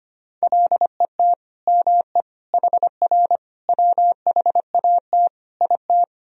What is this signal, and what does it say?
Morse code, heard as a keyed tone, about 25 wpm, 700 Hz. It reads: LET ME HR WHAT IT
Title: Sentences from Top 300 Words and Abbreviations 25wpm